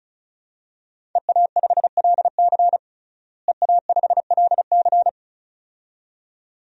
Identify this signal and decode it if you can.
Morse code, heard as a keyed tone, about 36 wpm, 700 Hz. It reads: EA5LC EA5LC
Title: DXCC-100 Call Signs - 5 Characters - Repeated 2x - 35wpm